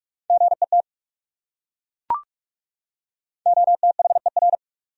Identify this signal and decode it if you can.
Morse code, heard as a keyed tone, about 40 wpm, 700 Hz. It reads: GET OTHER